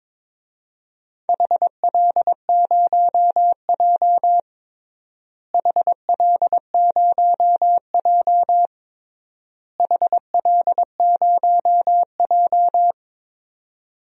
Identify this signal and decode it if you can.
Morse code, heard as a keyed tone, about 22 wpm, 700 Hz. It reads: HL0J HL0J HL0J
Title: DXCC-100 Call Signs - 4 Characters - Repeated 3x - 22wpm